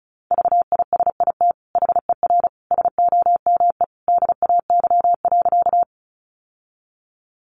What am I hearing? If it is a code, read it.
Morse code, heard as a keyed tone, about 34 wpm, 700 Hz. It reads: VISIT HER SOME DAY.
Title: Sentences from Top 900 Words 35wpm